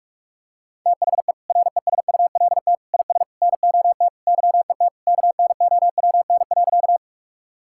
Morse code, heard as a keyed tone, about 41 wpm, 700 Hz. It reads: THE RESULT IS NOT YET KNOWN.
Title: Sentences from Top 1000 Words - Review Part 5 - 45wpm